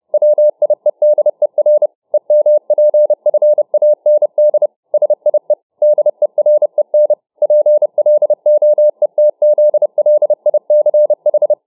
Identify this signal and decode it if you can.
Morse code, heard as a keyed tone, about 31 wpm, 600 Hz. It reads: WIEDER EMPFAND SIE DEREN PLOETZLICH